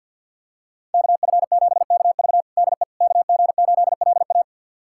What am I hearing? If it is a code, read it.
Morse code, heard as a keyed tone, about 52 wpm, 700 Hz. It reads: KF7KV DE KG8LA
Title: USA Callsign DE Callsign - 5 Characters - 50wpm